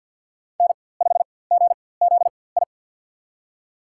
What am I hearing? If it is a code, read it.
Morse code, heard as a keyed tone, about 47 wpm, spaced as for 23 wpm, 700 Hz. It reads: N5GZI
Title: USA Call Signs - 5 Characters - 4x Character Spacing - 50wpm